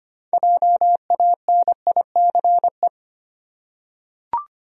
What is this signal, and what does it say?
Morse code, heard as a keyed tone, about 25 wpm, 700 Hz. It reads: JANICE
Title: USA Common Names - 25wpm